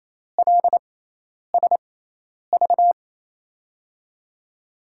Morse code, heard as a keyed tone, about 28 wpm, spaced as for 10 wpm, 700 Hz. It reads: LSV
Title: USA Call Signs - 6 Characters - 6x Character Spacing - 28wpm